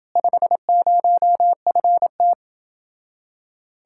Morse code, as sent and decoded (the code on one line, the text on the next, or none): ..... ----- ..-. -
50FT